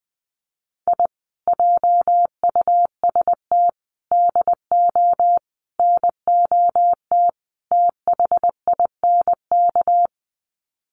Morse code, heard as a keyed tone, about 20 wpm, 700 Hz. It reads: I JUST DO NOT THINK
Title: Sets of 5 Words Encore 20wpm